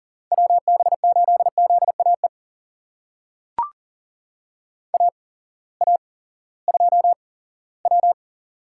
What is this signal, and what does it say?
Morse code, heard as a keyed tone, about 37 wpm, spaced as for 19 wpm, 700 Hz. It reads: WB8ZAE AA2W
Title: USA Call Signs - 8x Character Spacing - 40wpm